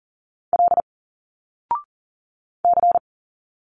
Morse code, heard as a keyed tone, about 41 wpm, 700 Hz. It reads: L C
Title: Single Letters 40wpm